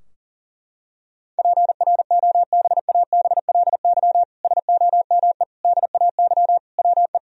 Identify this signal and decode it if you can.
Morse code, heard as a keyed tone, about 37 wpm, 700 Hz. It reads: PROBABLY SOME DAY WE
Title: Sentences from Top 1000 Words - Review Part 4 - 40wpm